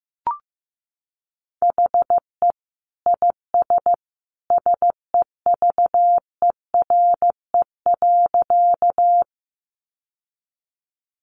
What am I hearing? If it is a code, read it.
Morse code, heard as a keyed tone, about 15 wpm, 700 Hz. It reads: HE IS SEVERE.
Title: Sentences from Top 1400 Words 15wpm